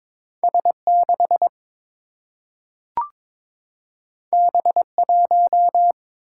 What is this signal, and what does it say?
Morse code, heard as a keyed tone, about 22 wpm, 700 Hz. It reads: S 6 B 1
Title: Single-Letter Single-Number 22wpm